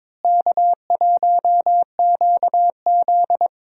Morse code, heard as a keyed tone, about 21 wpm, 700 Hz. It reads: K1QZ
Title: USA Call Signs - 4 Characters - Repeated 2x - 22wpm